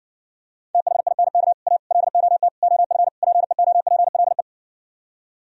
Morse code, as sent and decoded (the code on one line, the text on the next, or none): - .... .. -. -.- / .- / .-.. --- - / --- ..-. / .--. --- .--. .-.. .
THINK A LOT OF PEOPLE